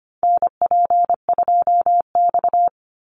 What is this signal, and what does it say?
Morse code, heard as a keyed tone, about 25 wpm, 700 Hz. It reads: NP2X
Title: Call Signs - 25wpm